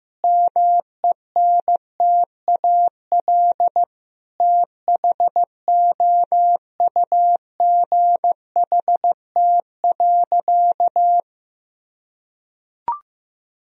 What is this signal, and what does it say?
Morse code, heard as a keyed tone, about 15 wpm, 700 Hz. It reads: MENTAL THOUGHT.
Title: Sentences from Top 1000 Words - Review Part 4 - 15wpm